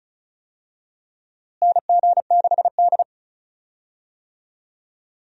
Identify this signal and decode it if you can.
Morse code, heard as a keyed tone, about 34 wpm, 700 Hz. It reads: NG6D
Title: USA Call Signs - 4 Characters - Repeated 3x - 35wpm